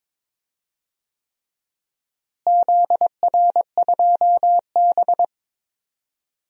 Morse code, heard as a keyed tone, about 22 wpm, 700 Hz. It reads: ZR2B
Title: DXCC-100 Call Signs - 4 Characters - Repeated 3x - 22wpm